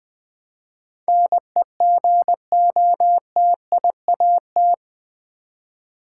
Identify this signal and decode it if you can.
Morse code, heard as a keyed tone, about 20 wpm, 700 Hz. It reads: NEGOTIAT
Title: Build A Word - 4 Letters At A Time - 11-Letter Words - 20wpm